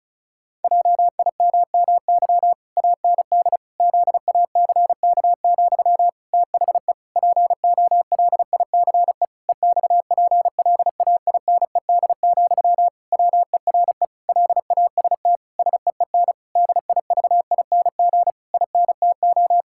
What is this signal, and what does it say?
Morse code, heard as a keyed tone, about 36 wpm, 700 Hz. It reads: JIMMY AND ZACK, THE POLICE EXPLAINED, WERE LAST SEEN DIVING INTO
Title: Pangram Challenge v2 - 35wpm